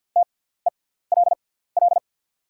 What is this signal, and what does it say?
Morse code, heard as a keyed tone, about 48 wpm, spaced as for 17 wpm, 700 Hz. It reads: TELL
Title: 6x Character Spacing - Top 300 Words - 50wpm